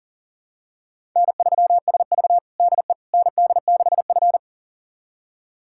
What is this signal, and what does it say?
Morse code, heard as a keyed tone, about 40 wpm, 700 Hz. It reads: N3SV DE ND6F
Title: USA Callsign DE Callsign - 4 Characters - 40wpm